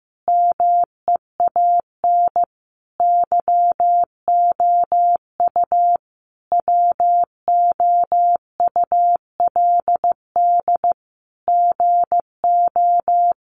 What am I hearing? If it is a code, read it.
Morse code, heard as a keyed tone, about 14 wpm, 700 Hz. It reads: MEAN YOU WOULD GO
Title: Sentences from Top 600 Words 15wpm